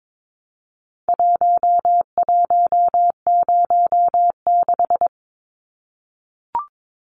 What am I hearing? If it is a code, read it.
Morse code, heard as a keyed tone, about 22 wpm, 700 Hz. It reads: 1106
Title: Numbers - Contest Style - 4-Digit 22wpm